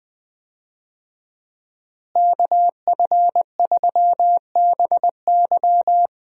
Morse code, heard as a keyed tone, about 20 wpm, 700 Hz. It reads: KF3BY
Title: USA Call Signs - 5 Characters - Repeated 3x - 20wpm